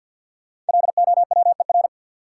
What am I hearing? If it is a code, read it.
Morse code, heard as a keyed tone, about 52 wpm, 700 Hz. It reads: LOWER
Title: Sentences from Top 700 Words - New Word Drill - 50wpm